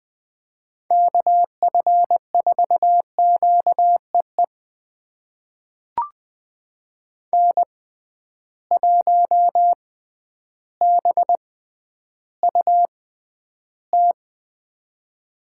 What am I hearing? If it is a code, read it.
Morse code, heard as a keyed tone, about 20 wpm, spaced as for 11 wpm, 700 Hz. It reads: KF4QEE N1BUT
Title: USA Call Signs - 6x Character Spacing - 20wpm